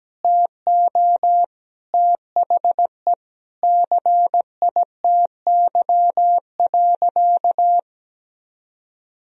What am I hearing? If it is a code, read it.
Morse code, heard as a keyed tone, about 17 wpm, 700 Hz. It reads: TO THE CITY.
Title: Sentences from Top 1000 Words - Review Part 1 - 17wpm